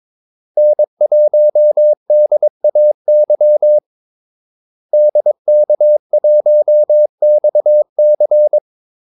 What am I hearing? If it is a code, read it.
Morse code, heard as a keyed tone, about 22 wpm, 600 Hz. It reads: N1DAY DK1XC
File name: Callsigns-22wpm